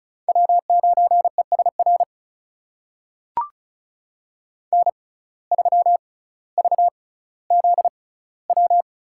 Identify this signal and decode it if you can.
Morse code, heard as a keyed tone, about 36 wpm, spaced as for 18 wpm, 700 Hz. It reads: W9ESR N3VZW
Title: USA Call Signs - 5 Characters - 6x Character Spacing - 35wpm